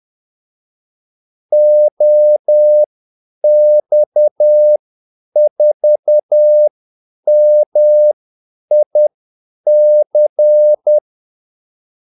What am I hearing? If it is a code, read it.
Morse code, heard as a keyed tone, about 10 wpm, spaced as for 8 wpm, 600 Hz. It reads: OX4MIC
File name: Callsigns-10-8wpm